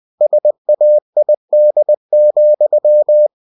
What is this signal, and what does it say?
Morse code, heard as a keyed tone, about 20 wpm, 600 Hz. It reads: SAID,
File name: cw-books / winnie-the-pooh / winnie-20wpm-page-0019